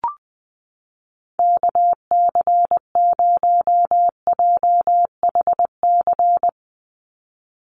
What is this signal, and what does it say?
Morse code, heard as a keyed tone, about 20 wpm, 700 Hz. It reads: KC0JHC
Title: USA Call Signs 20wpm